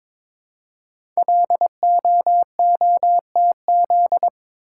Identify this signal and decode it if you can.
Morse code, heard as a keyed tone, about 22 wpm, 700 Hz. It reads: LOOTZ
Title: Code Groups - 5 Letters - 22wpm